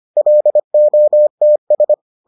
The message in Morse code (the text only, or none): LOTS